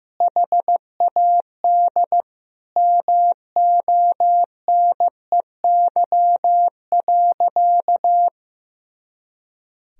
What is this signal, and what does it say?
Morse code, heard as a keyed tone, about 15 wpm, 700 Hz. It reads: HAD MONEY.
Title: Sentences from Top 500 Words and Abbreviations 15wpm